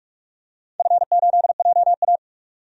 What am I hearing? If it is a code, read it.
Morse code, heard as a keyed tone, about 41 wpm, 700 Hz. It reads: F8JA
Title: DXCC-100 Call Signs - 4 Characters - 45wpm